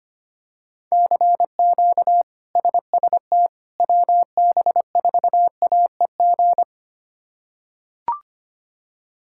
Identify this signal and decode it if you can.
Morse code, heard as a keyed tone, about 25 wpm, 700 Hz. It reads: CQ SST WB4AEG
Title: K1USN SST QSO Practice - 25wpm